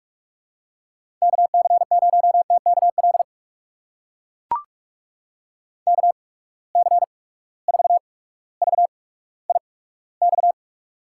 Morse code, heard as a keyed tone, about 43 wpm, spaced as for 19 wpm, 700 Hz. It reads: KC0TKL KC4VIX
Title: USA Call Signs - 6 Characters - 8x Character Spacing - 45wpm